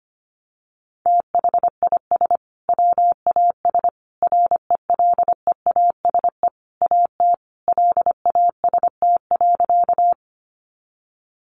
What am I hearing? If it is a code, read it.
Morse code, heard as a keyed tone, about 25 wpm, 700 Hz. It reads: THIS WAS RELEASE AT LAST.
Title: Sentences from Top 1100 Words 25wpm